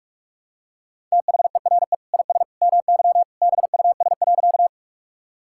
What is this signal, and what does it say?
Morse code, heard as a keyed tone, about 41 wpm, 700 Hz. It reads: THERE IS MY BUS.
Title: Sentences from Top 1000 Words - Review Part 6 - 45wpm